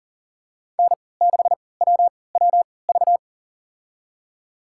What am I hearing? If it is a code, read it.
Morse code, heard as a keyed tone, about 40 wpm, spaced as for 23 wpm, 700 Hz. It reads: N6WWV
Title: USA Call Signs - 5 Characters - 3x Character Spacing - 40wpm